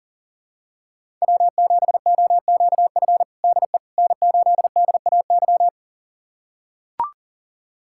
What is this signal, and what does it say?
Morse code, heard as a keyed tone, about 40 wpm, 700 Hz. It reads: W7OQF DE N8DAY
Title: USA Callsign DE Callsign - 5 Characters - 40wpm